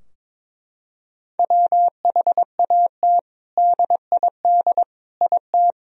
Morse code, heard as a keyed tone, about 22 wpm, 700 Hz. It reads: WHAT DID IT
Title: Sentences from Top 1000 Words - Review Part 2 - 22wpm